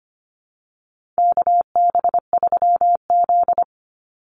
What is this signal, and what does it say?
Morse code, heard as a keyed tone, about 25 wpm, 700 Hz. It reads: KB3Z
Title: Call Signs Repeated 2x - 25wpm